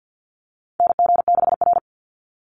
Morse code, heard as a keyed tone, about 49 wpm, 700 Hz. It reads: NG6R
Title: NA SOTA Chaser Callsigns 50wpm